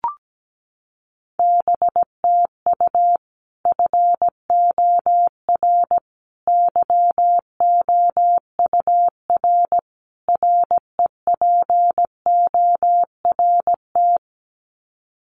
Morse code, heard as a keyed tone, about 17 wpm, 700 Hz. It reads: BTU FOR YOUR REPORT
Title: Common QSO Phrases 17wpm